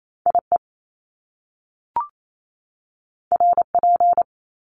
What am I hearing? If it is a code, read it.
Morse code, heard as a keyed tone, about 28 wpm, 700 Hz. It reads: IE RP
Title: Top Two Letter Combinations in Words 28wpm